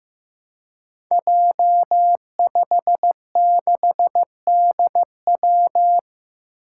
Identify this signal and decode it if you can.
Morse code, heard as a keyed tone, about 15 wpm, 700 Hz. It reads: J56DW
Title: Five Challenging Letter-Number Combinations 15wpm